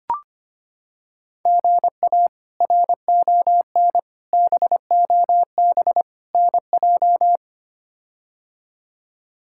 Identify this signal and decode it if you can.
Morse code, heard as a keyed tone, about 25 wpm, 700 Hz. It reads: GA RON BOB NJ
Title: K1USN SST QSO Practice - 25wpm